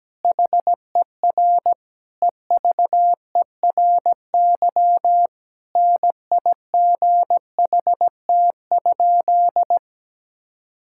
Morse code, heard as a keyed tone, about 17 wpm, 700 Hz. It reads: HER EVERY NIGHT?
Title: Sentences from Top 1000 Words - Review Part 5 - 17wpm